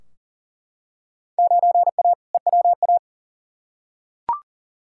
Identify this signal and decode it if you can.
Morse code, heard as a keyed tone, about 40 wpm, 700 Hz. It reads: 9A EWA
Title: Field Day Exchanges Encore - 40wpm